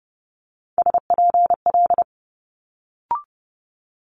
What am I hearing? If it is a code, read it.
Morse code, heard as a keyed tone, about 30 wpm, 700 Hz. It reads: SPL
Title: Multi-letter Phonemes Quick Fire 30wpm